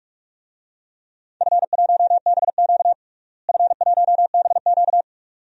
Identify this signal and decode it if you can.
Morse code, heard as a keyed tone, about 47 wpm, 700 Hz. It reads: F1BQ F1BQ
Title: DXCC-100 Call Signs - 4 Characters - Repeated 2x - 45wpm